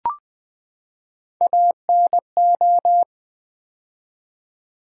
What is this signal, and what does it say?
Morse code, heard as a keyed tone, about 20 wpm, 700 Hz. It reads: ANO
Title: Build A Word - 3 Letters At A Time - 9-Letter Words - 20wpm